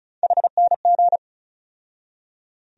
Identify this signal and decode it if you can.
Morse code, heard as a keyed tone, about 34 wpm, 700 Hz. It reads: HNG
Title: USA Call Sign Postfixes - 3 Letters 35wpm